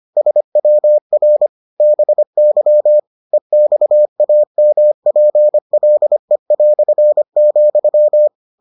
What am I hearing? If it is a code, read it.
Morse code, heard as a keyed tone, about 25 wpm, 600 Hz. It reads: SWR BY EXAMPLE",